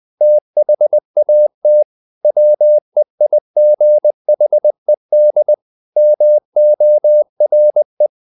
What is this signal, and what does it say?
Morse code, heard as a keyed tone, about 20 wpm, 600 Hz. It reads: THAT WEIGHED MORE
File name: the old man and the sea / the old man and the sea part 1 20wpm